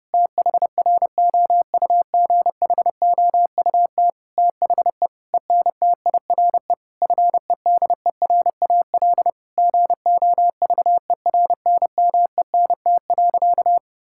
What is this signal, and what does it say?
Morse code, heard as a keyed tone, about 30 wpm, 700 Hz. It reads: THROUGHOUT THE ENTIRE FEDERAL GOVERNMENT.
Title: Sentences from Top 1000 Words - Review Part 5 - 30wpm